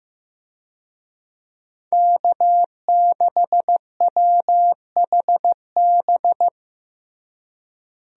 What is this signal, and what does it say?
Morse code, heard as a keyed tone, about 15 wpm, 700 Hz. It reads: K6WHB
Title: USA Call Signs - 5 Characters - Repeated 2x - 15wpm